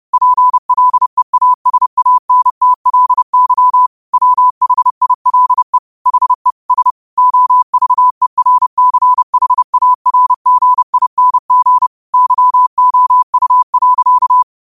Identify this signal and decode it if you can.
Morse code, heard as a keyed tone, about 30 wpm, 1000 Hz. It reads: PLEASANTLY WHILE HE S OVERCHARGING YOU.
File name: qotdfiles / 30wpm